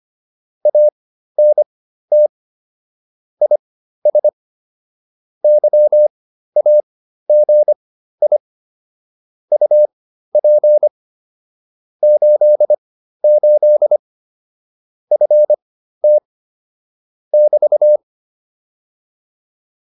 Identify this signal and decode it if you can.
Morse code, heard as a keyed tone, about 25 wpm, spaced as for 13 wpm, 600 Hz. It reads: ANT IS YAGI UP 88 FT <BT>
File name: QSO303 13